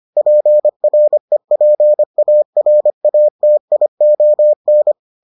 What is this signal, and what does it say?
Morse code, heard as a keyed tone, about 25 wpm, 600 Hz. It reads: PREPARATION